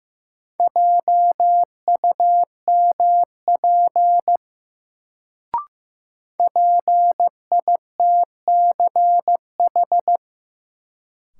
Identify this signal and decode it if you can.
Morse code, heard as a keyed tone, about 15 wpm, 700 Hz. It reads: JUMP PITCH
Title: Sentences from Top 1300 Words - New Word Drill Alt - 15wpm